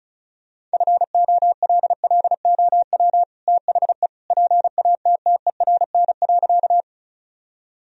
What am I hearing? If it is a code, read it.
Morse code, heard as a keyed tone, about 34 wpm, 700 Hz. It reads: FOLLOW THE PATTERN.
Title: Sentences from Top 1000 Words - Review Part 2 - 35wpm